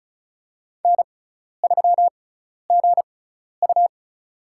Morse code, heard as a keyed tone, about 36 wpm, spaced as for 12 wpm, 700 Hz. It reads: N3GU
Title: USA Call Signs - 4 Characters - 6x Character Spacing - 35wpm